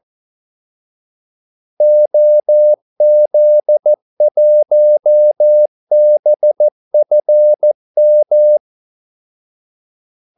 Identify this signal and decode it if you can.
Morse code, heard as a keyed tone, about 14 wpm, 600 Hz. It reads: OZ1BFM
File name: Callsigns-14wpm